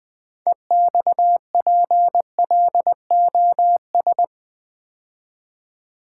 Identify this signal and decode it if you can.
Morse code, heard as a keyed tone, about 20 wpm, 700 Hz. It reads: EXPLOS
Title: Build A Word - 3 Letters At A Time - 9-Letter Words - 20wpm